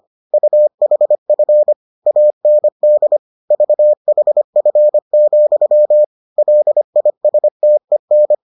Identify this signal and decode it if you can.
Morse code, heard as a keyed tone, about 25 wpm, 600 Hz. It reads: UHF AND VHF, LISTEN